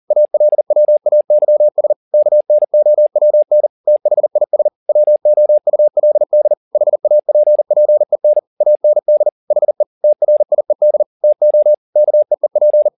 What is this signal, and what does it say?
Morse code, heard as a keyed tone, about 40 wpm, 600 Hz. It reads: ALWAYS KNOWN THIS WOULD HAPPEN AND HE TRIED TO KEEP
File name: the old man and the sea 2 40wpm